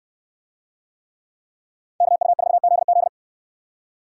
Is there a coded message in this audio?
Yes: Morse code from 2.0 to 3.1 s, heard as a keyed tone, about 69 wpm, 700 Hz. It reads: KU4ZZ